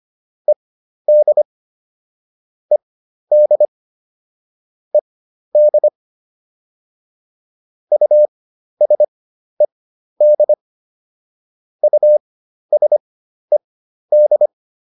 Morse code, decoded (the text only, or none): ED ED ED USED USED